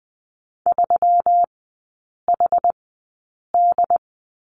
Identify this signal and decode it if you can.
Morse code, heard as a keyed tone, about 19 wpm, spaced as for 8 wpm, 700 Hz. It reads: 3HD
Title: 121 Lesson 20e - Number 3 - Three Characters 20wpm